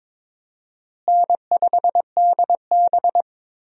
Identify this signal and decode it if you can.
Morse code, heard as a keyed tone, about 22 wpm, 700 Hz. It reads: N5DB
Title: USA Call Signs - 4 Characters - Repeated 3x - 22wpm